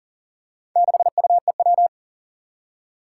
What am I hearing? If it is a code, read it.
Morse code, heard as a keyed tone, about 38 wpm, 700 Hz. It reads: 6UEW
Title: Four Challenging Letter-Number Combinations 40wpm